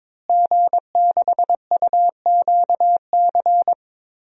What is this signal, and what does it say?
Morse code, heard as a keyed tone, about 22 wpm, 700 Hz. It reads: G6UQC
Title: DXCC-100 Call Signs - 5 Characters - Repeated 2x - 22wpm